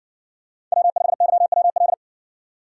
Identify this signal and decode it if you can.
Morse code, heard as a keyed tone, about 60 wpm, 700 Hz. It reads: W5YWL